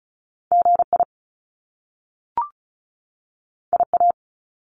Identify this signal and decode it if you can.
Morse code, heard as a keyed tone, about 36 wpm, 700 Hz. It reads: GI IA
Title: Top Two Letter Combinations in Words 35wpm